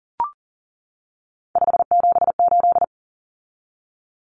Morse code, heard as a keyed tone, about 40 wpm, 700 Hz. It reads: 578
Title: Numbers - 3-Digit 40wpm